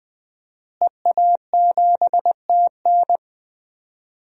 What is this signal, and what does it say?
Morse code, heard as a keyed tone, about 20 wpm, 700 Hz. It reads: EA7TN
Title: DXCC-100 Call Signs - 5 Characters - Repeated 2x - 20wpm